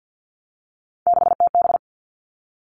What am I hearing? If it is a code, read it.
Morse code, heard as a keyed tone, about 50 wpm, 700 Hz. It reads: 6TB
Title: Three Challenging Letter-Number Combinations 50wpm